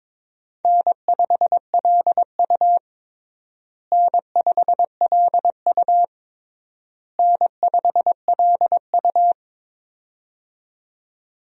Morse code, heard as a keyed tone, about 22 wpm, 700 Hz. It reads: N5LU N5LU N5LU
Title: USA Call Signs - 4 Characters - Repeated 3x - 22wpm